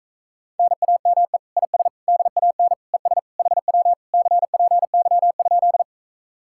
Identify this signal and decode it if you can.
Morse code, heard as a keyed tone, about 37 wpm, 700 Hz. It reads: NAME IS DAN ES HW CPY?